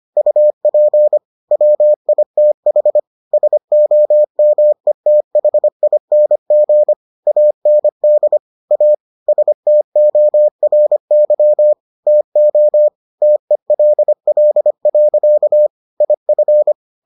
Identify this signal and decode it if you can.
Morse code, heard as a keyed tone, about 25 wpm, 600 Hz. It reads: UP WITH SOMETHING AND A STORY TO TELL. IF